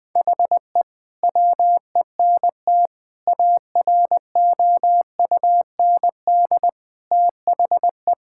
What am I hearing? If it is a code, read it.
Morse code, heard as a keyed tone, about 19 wpm, 700 Hz. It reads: HE WENT AROUND THE
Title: Sentences from Top 1000 Words - Review Part 5 - 20wpm